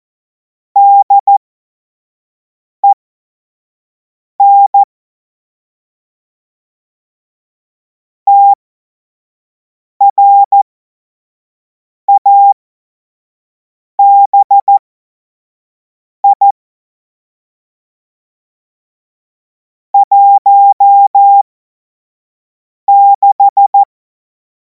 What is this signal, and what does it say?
Morse code, heard as a keyed tone, about 14 wpm, spaced as for 5 wpm, 800 Hz. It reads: DEN TRABI 16
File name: Morsekurs / Witze / Witz 07. April 2026 25 F70